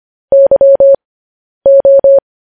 Morse code, heard as a keyed tone, about 25 wpm, spaced as for 10 wpm, 550 Hz. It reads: YO